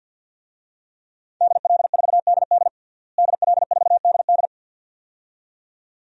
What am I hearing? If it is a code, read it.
Morse code, heard as a keyed tone, about 51 wpm, 700 Hz. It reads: DL4DD DL4DD
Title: DXCC-100 Call Signs - 5 Characters - Repeated 2x - 50wpm